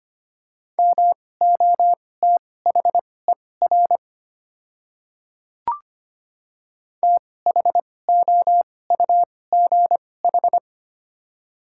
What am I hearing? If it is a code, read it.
Morse code, heard as a keyed tone, about 25 wpm, spaced as for 18 wpm, 700 Hz. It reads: MOTHER THOUGH
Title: 2x Character Spacing - Top 500 Words - 25wpm